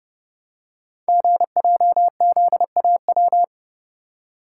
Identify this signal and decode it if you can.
Morse code, heard as a keyed tone, about 29 wpm, 700 Hz. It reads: GJZAW
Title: Code Groups - 5 Letters - 30wpm